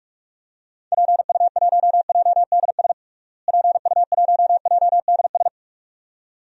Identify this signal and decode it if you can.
Morse code, heard as a keyed tone, about 41 wpm, 700 Hz. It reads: PU1JDS PU1JDS
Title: DXCC-100 Call Signs - 6 Characters - Repeated 2x - 45wpm